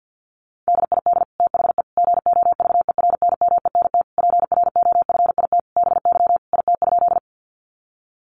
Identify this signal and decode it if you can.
Morse code, heard as a keyed tone, about 50 wpm, 700 Hz. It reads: DID THE GOVERNMENT PROFIT BY IT?
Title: Sentences from Top 800 Words 50wpm